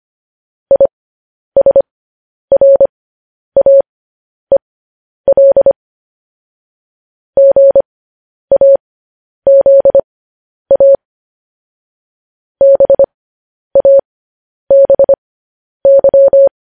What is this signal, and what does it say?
Morse code, heard as a keyed tone, about 25 wpm, spaced as for 10 wpm, 550 Hz. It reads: ISRAEL GAZA BABY